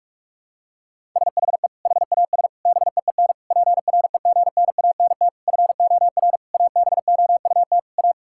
Morse code, heard as a keyed tone, about 46 wpm, 700 Hz. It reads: SHE HAS BEEN PREGNANT FOR ABOUT A
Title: Sentences from Top 1000 Words - Review Part 6 - 45wpm